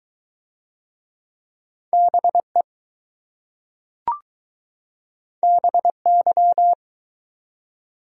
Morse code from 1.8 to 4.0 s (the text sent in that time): BE